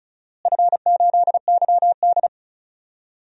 Morse code, heard as a keyed tone, about 36 wpm, 690 Hz. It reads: F8YD
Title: DXCC-100 Call Signs - 4 Characters - Repeated 2x - 35wpm